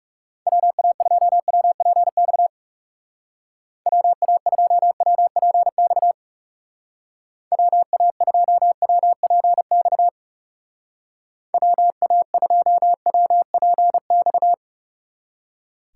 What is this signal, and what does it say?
Morse code, heard as a keyed tone, about 38 wpm, 700 Hz. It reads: WA2WPX WA2WPX WA2WPX WA2WPX